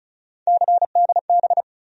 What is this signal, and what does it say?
Morse code, heard as a keyed tone, about 36 wpm, 700 Hz. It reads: CDB
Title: USA Call Sign Postfixes - 3 Letters 35wpm